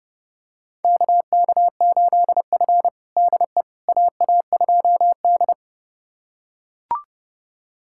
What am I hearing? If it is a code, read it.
Morse code, heard as a keyed tone, about 32 wpm, 700 Hz. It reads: KK8F DE AA2D
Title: USA Callsign DE Callsign - 4 Characters - 30wpm